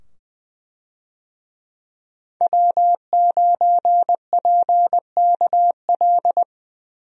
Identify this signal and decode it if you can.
Morse code, heard as a keyed tone, about 20 wpm, 700 Hz. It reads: W9PKL
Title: USA Call Signs - 5 Characters - 20wpm